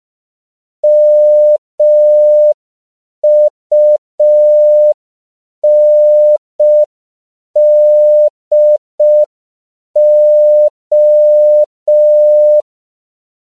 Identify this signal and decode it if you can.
Morse code, heard as a keyed tone, about 5 wpm, 600 Hz. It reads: MUNDO
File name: CW / 5PPM